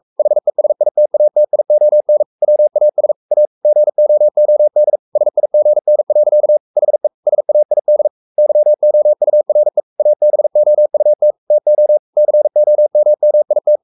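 Morse code, heard as a keyed tone, about 44 wpm, 600 Hz. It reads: HESITATION WAS A GOOD SIGN. HE SAID YOURE ABOUT TO COMMIT
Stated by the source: Morse code at 43 wpm sending Ken Follett's novel Never in English, from CW med Gnister.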